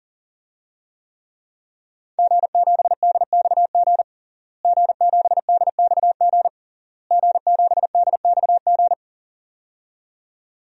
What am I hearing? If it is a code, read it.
Morse code, heard as a keyed tone, about 37 wpm, 700 Hz. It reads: G7DXG G7DXG G7DXG
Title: DXCC-100 Call Signs - 5 Characters - Repeated 3x - 40wpm